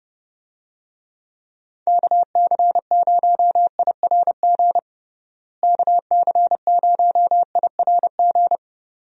Morse code, heard as a keyed tone, about 30 wpm, 700 Hz. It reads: KC0IRG KC0IRG
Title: USA Call Signs - 6 Characters - Repeated 2x - 30wpm